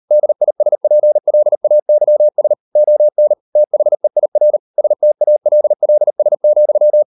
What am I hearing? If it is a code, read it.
Morse code, heard as a keyed tone, about 38 wpm, 600 Hz. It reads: DISPLAYS ON THEIR STALLS,